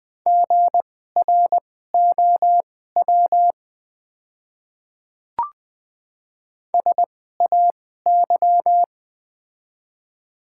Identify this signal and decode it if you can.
Morse code, heard as a keyed tone, about 20 wpm, spaced as for 14 wpm, 700 Hz. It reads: GROW SAY
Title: 2x Character Spacing - Top 300 Words - 20wpm